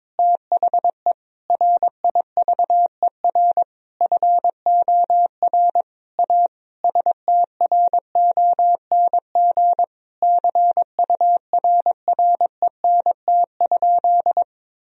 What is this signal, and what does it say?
Morse code, heard as a keyed tone, about 22 wpm, 700 Hz. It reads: THE RIVER FOR A STRONG CURRENT?
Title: Sentences from Top 1000 Words - Review Part 4 - 22wpm